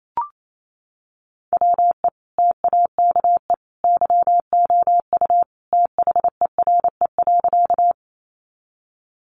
Morse code, heard as a keyed tone, about 28 wpm, 700 Hz. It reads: WE TAKE YOU THERE.
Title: A Sound Of Thunder 28wpm